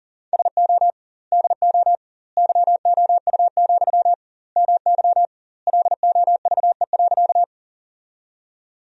Morse code, heard as a keyed tone, about 40 wpm, 700 Hz. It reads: SO DO YOU, MY LOVE.
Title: 2x Word-Spacing - Sentences from Top 400 Words - 40wpm